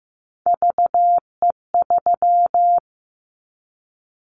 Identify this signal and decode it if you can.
Morse code, heard as a keyed tone, about 15 wpm, 700 Hz. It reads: VE3
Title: Callsign Prefixes for Top 100 DXCC - 15wpm